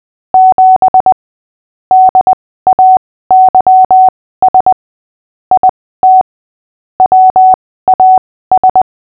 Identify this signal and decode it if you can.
Morse code, heard as a keyed tone, about 20 wpm, spaced as for 15 wpm, 750 Hz. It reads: 7 DAYS IT WAS
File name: cw / fists / Short-Story-5-15wpm